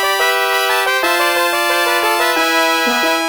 There is no code.